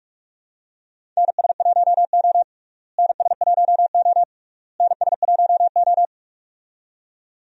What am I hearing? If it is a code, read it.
Morse code, heard as a keyed tone, about 43 wpm, 700 Hz. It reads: NS1O NS1O NS1O
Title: USA Call Signs - 4 Characters - Repeated 3x - 45wpm